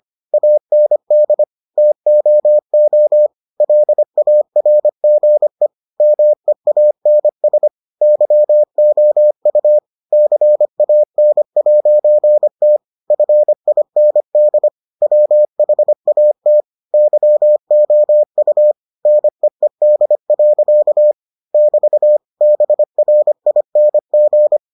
Morse code, heard as a keyed tone, about 25 wpm, 600 Hz. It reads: AND TOO LARGE MEANS YOU CAN'T FIND WHAT YOU NEED. <BT> BRING